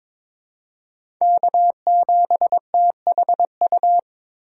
Morse code, heard as a keyed tone, about 22 wpm, 700 Hz. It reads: K7THU
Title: USA Call Signs - 5 Characters - 3x Character Spacing - 22wpm